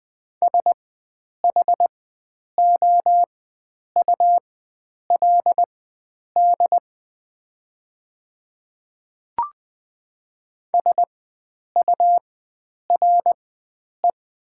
Morse code, heard as a keyed tone, about 20 wpm, spaced as for 9 wpm, 690 Hz. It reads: SHOULD SURE
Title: 4x Character Spacing - Top 300 Words - 20wpm